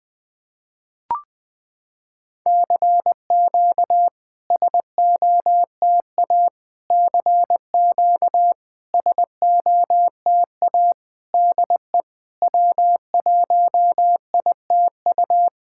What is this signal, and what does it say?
Morse code, heard as a keyed tone, about 20 wpm, 700 Hz. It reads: CQ SOTA CQ SOTA DE W1ITU
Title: SOTA QSO Practice 20wpm